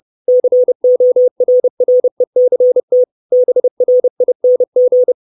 Morse code, heard as a keyed tone, about 30 wpm, 500 Hz. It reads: CORRECT BRING
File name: 632 Words 30WPM